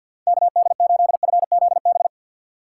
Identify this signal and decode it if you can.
Morse code, heard as a keyed tone, about 52 wpm, 690 Hz. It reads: KD8FZB
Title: USA Call Signs - 6x Character Spacing - 50wpm